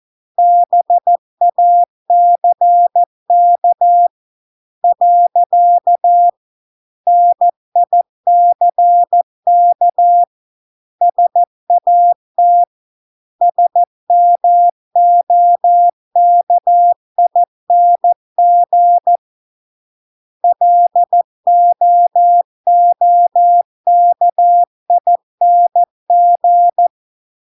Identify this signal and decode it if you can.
Morse code, heard as a keyed tone, about 14 wpm, 700 Hz. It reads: BACK . NICK SAT SMOKING LOOKING